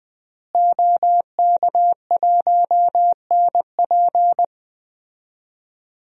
Morse code, heard as a keyed tone, about 21 wpm, 700 Hz. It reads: OK1NP
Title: DXCC-100 Call Signs - 5 Characters - Repeated 3x - 20wpm